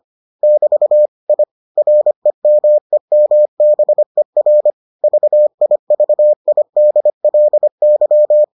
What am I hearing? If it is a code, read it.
Morse code, heard as a keyed tone, about 25 wpm, 600 Hz. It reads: <BT> I REMEMBER VIVIDLY